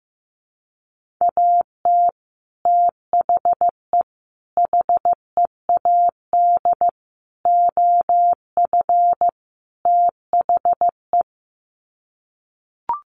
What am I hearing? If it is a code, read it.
Morse code, heard as a keyed tone, about 15 wpm, 700 Hz. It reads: AT THE HEAD OF THE
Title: Sets of 5 Words 15wpm